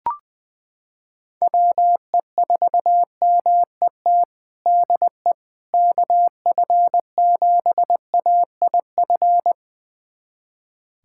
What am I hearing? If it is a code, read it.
Morse code, heard as a keyed tone, about 20 wpm, 700 Hz. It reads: WE4MET DE KF7AIF
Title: USA Callsign DE Callsign - 6 Characters - 20wpm